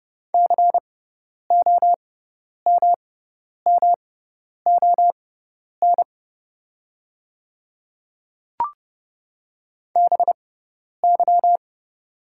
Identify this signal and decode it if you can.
Morse code, heard as a keyed tone, about 31 wpm, spaced as for 10 wpm, 690 Hz. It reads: COMMON BY
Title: 6x Character Spacing - Top 500 Words - 30wpm